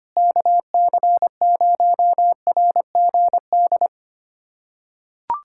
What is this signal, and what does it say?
Morse code, heard as a keyed tone, about 25 wpm, 700 Hz. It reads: KC0RGD
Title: USA Callsign DE Callsign - 6 Characters - 25wpm